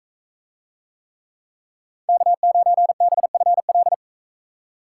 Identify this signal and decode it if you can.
Morse code, heard as a keyed tone, about 41 wpm, 700 Hz. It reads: K9BFL